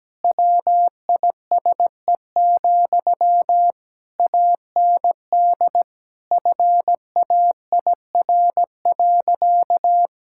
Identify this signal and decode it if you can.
Morse code, heard as a keyed tone, about 17 wpm, 700 Hz. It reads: WISE, AND FAIR.
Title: Sentences from Top 1600 Words 17wpm